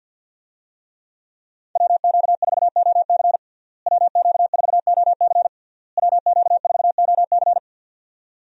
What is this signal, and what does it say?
Morse code, heard as a keyed tone, about 49 wpm, 700 Hz. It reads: WQ4OC WQ4OC WQ4OC
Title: USA Call Signs - 5 Characters - Repeated 3x - 50wpm